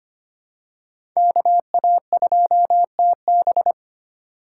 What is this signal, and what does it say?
Morse code, heard as a keyed tone, about 25 wpm, 700 Hz. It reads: KA2TB
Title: Code Groups - 5 Letter:Numbers - 25wpm